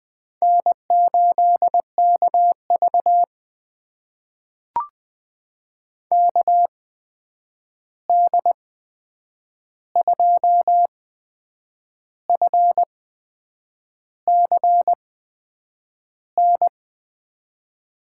Morse code, heard as a keyed tone, about 20 wpm, spaced as for 8 wpm, 700 Hz. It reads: N8KV KD2FCN
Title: USA Call Signs - 8x Character Spacing - 20wpm